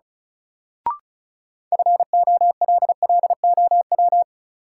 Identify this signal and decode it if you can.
Morse code, heard as a keyed tone, about 36 wpm, 700 Hz. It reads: FOLLOW
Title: Sentences from Top 200 Words - New Word Drill - 35wpm